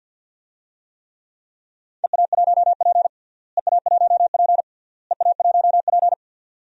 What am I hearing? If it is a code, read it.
Morse code, heard as a keyed tone, about 47 wpm, 700 Hz. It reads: EA1P EA1P EA1P
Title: DXCC-100 Call Signs - 4 Characters - Repeated 3x - 50wpm